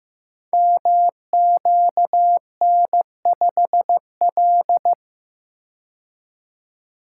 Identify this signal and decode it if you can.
Morse code, heard as a keyed tone, about 15 wpm, 700 Hz. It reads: MQN5L
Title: Code Groups - 5 Letter:Numbers - 15wpm